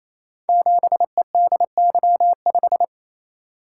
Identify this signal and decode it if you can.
Morse code, heard as a keyed tone, about 29 wpm, 700 Hz. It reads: 7EDY5
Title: Code Groups - 5 Letter:Numbers - 28wpm